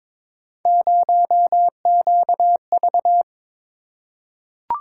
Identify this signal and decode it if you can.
Morse code, heard as a keyed tone, about 22 wpm, 700 Hz. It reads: 0QV
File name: practice / 22wpm / Three Letter-Number Combinations - 22wpm